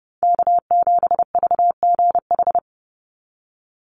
Morse code, heard as a keyed tone, about 30 wpm, 700 Hz. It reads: K7VGH
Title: USA Call Signs 30wpm